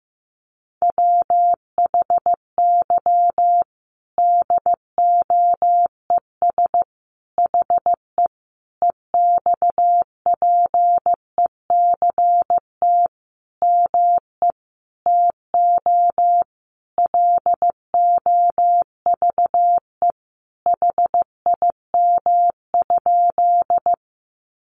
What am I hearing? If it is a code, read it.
Morse code, heard as a keyed tone, about 15 wpm, 700 Hz. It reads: WHY DOES HE EXPECT ME TO LOVE HIM?
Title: Sentences from Top 1100 Words 15wpm